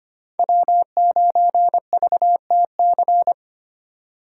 Morse code, heard as a keyed tone, about 32 wpm, 700 Hz. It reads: W9VTC